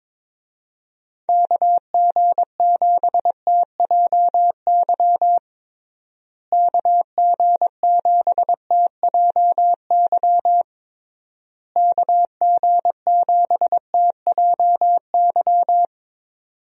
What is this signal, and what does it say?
Morse code, heard as a keyed tone, about 22 wpm, 700 Hz. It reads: KG7TJY KG7TJY KG7TJY
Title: USA Call Signs - 6 Characters - Repeated 3x - 22wpm